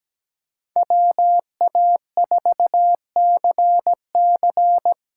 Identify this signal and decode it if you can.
Morse code, heard as a keyed tone, about 17 wpm, 700 Hz. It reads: WA4CC
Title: USA Callsign DE Callsign - 5 Characters - 17wpm